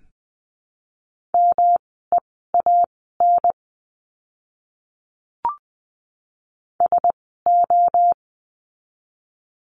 Morse code, heard as a keyed tone, about 20 wpm, spaced as for 14 wpm, 700 Hz. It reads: MEAN SO
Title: 230 Lesson 37f - Letter x - Words 20wpm